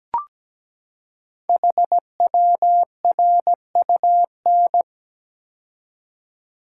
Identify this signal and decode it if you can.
Morse code, heard as a keyed tone, about 17 wpm, 700 Hz. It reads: HWRUN
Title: Code Groups - 5 Letters - 17wpm